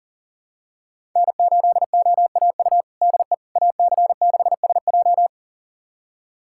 Morse code, heard as a keyed tone, about 40 wpm, 700 Hz. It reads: N8OAU DE AC6SJ
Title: USA Callsign DE Callsign - 5 Characters - 40wpm